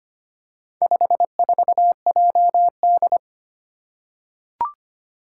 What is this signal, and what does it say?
Morse code, heard as a keyed tone, about 25 wpm, 700 Hz. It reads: 54JD
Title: Four Challenging Letter-Number Combinations 25wpm